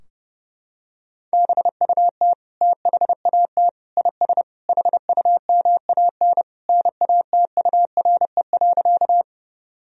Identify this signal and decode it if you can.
Morse code, heard as a keyed tone, about 30 wpm, 700 Hz. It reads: BUT THAT IS HUMAN NATURE.
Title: Sentences from Top 1000 Words - Review Part 3 - 30wpm